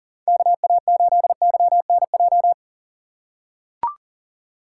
Morse code, heard as a keyed tone, about 40 wpm, 700 Hz. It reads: KA8YNJ
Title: USA Call Signs - 6 Characters - 6x Character Spacing - 40wpm